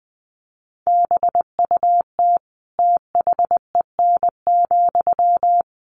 Morse code, heard as a keyed tone, about 20 wpm, 700 Hz. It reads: BUT THEN,
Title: Sentences from Top 1400 Words 20wpm